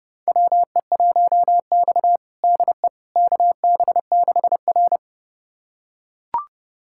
Morse code, heard as a keyed tone, about 31 wpm, 700 Hz. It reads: WE1X DE KB6R
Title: USA Callsign DE Callsign - 4 Characters - 30wpm